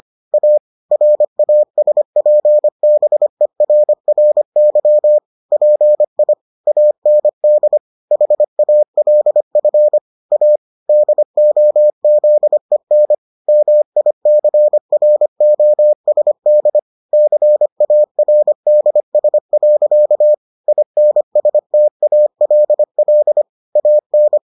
Morse code, heard as a keyed tone, about 25 wpm, 600 Hz. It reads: A RASPBERRY PI AND HALF A DOZEN MICROSD CARDS. INSTALL AN